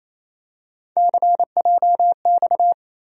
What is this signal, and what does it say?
Morse code, heard as a keyed tone, about 28 wpm, 700 Hz. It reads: CJX
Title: USA Call Sign Postfixes - 3 Letters 28wpm